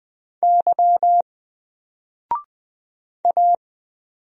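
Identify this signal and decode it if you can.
Morse code, heard as a keyed tone, about 20 wpm, 700 Hz. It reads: Y A
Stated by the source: Single letter and number Morse drill at 20 wpm, from N4LCD.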